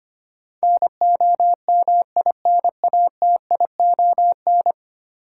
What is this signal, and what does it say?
Morse code, heard as a keyed tone, about 25 wpm, 700 Hz. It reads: NOMINATION
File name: 10 Letter Words - First 3 Letters Given - 25wpm